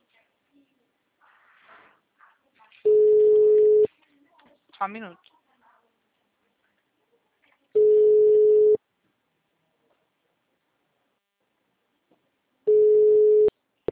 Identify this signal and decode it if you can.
Ringback tone (Europe, 425 Hz), with speech.
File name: out- -2020- - - .2